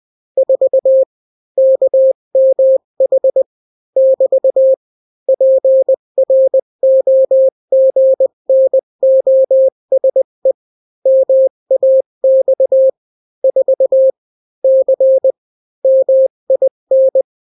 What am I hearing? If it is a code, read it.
Morse code, heard as a keyed tone, about 20 wpm, 530 Hz. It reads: 4 KMH <BT> PROGNOSE MAX 4 C MIN
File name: daily / 18wpm